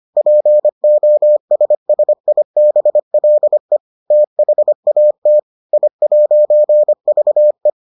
Morse code, heard as a keyed tone, about 25 wpm, 600 Hz. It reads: POSSIBLE THAT I'VE